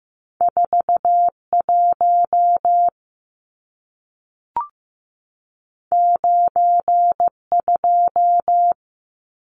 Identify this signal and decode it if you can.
Morse code, heard as a keyed tone, about 15 wpm, 700 Hz. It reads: 41 92
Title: Numbers - 2-Digit 15wpm